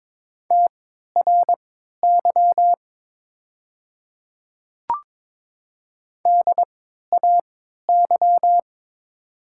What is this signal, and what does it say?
Morse code, heard as a keyed tone, about 22 wpm, spaced as for 13 wpm, 700 Hz. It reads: TRY DAY
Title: 3x Character Spacing - Top 200 Words - 22wpm